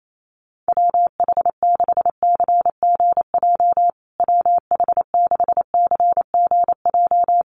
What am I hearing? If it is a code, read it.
Morse code, heard as a keyed tone, about 28 wpm, 700 Hz. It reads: WH6CGJ WH6CGJ
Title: USA Call Signs Repeated 3x 28wpm